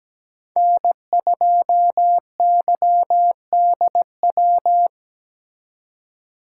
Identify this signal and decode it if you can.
Morse code, heard as a keyed tone, about 20 wpm, 700 Hz. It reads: N2YDW
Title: Speed Racing - USA Call Signs 17wpm